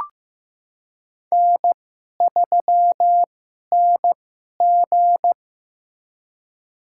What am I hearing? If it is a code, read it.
Morse code, heard as a keyed tone, about 16 wpm, spaced as for 11 wpm, 700 Hz. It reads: N3NG
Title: USA Call Signs - 4 Characters - 2x Character Spacing - 15wpm